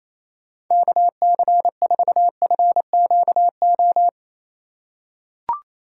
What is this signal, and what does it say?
Morse code, heard as a keyed tone, about 28 wpm, 700 Hz. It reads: KC4FQO
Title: USA Call Signs - 6 Characters - 4x Character Spacing - 28wpm